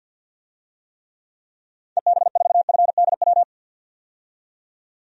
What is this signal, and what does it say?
Morse code, heard as a keyed tone, about 50 wpm, 700 Hz. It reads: EB4FDW
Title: DXCC-100 Call Signs - 6 Characters - 50wpm